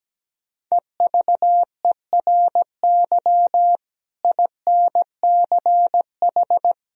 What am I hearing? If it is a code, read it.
Morse code, heard as a keyed tone, about 17 wpm, 700 Hz. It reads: EVERY INCH
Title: Sentences from Top 1000 Words - Review Part 3 - 17wpm